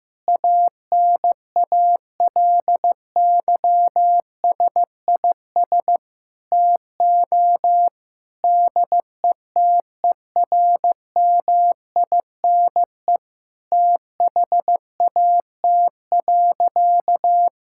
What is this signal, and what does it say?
Morse code, heard as a keyed tone, about 15 wpm, 700 Hz. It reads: ANALYSIS TO DETERMINE THAT.
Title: Sentences from Top 1000 Words - Review Part 5 - 15wpm